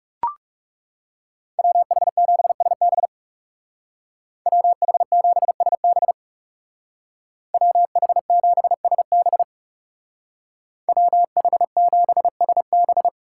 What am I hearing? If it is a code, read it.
Morse code, heard as a keyed tone, about 36 wpm, 700 Hz. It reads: WH7SB WH7SB WH7SB WH7SB
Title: Speed Racing - USA 2x2 Call Signs 30wpm